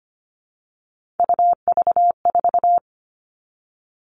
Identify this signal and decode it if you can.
Morse code, heard as a keyed tone, about 25 wpm, 700 Hz. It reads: UV4
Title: All Cut Numbers - 3-Digit 25wpm